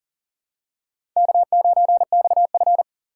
Morse code, heard as a keyed tone, about 40 wpm, 700 Hz. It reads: K9XF
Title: USA Call Signs - 4 Characters - 3x Character Spacing - 40wpm